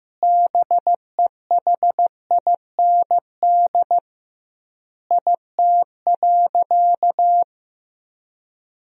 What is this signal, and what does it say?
Morse code, heard as a keyed tone, about 15 wpm, 700 Hz. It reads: BEHIND IT.
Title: 2x Word-Spacing - Sentences from Top 500 Words - 15wpm